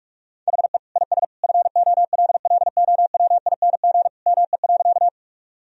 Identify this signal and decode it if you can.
Morse code, heard as a keyed tone, about 45 wpm, 700 Hz. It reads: HE IS FOLLOWING ME.
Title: Sentences from Top 1000 Words - Review Part 4 - 45wpm